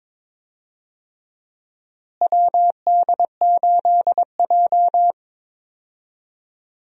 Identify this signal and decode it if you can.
Morse code, heard as a keyed tone, about 22 wpm, 700 Hz. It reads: WD8J
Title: USA Call Signs - 4 Characters - Repeated 3x - 22wpm